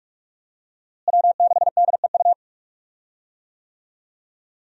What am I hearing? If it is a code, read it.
Morse code, heard as a keyed tone, about 46 wpm, 700 Hz. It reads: W6DEU
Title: USA Call Signs - 5 Characters - Repeated 3x - 45wpm